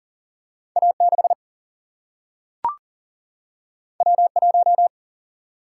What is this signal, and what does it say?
Morse code, heard as a keyed tone, about 40 wpm, 700 Hz. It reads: A 6 W 1
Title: Single-Letter Single-Number 40wpm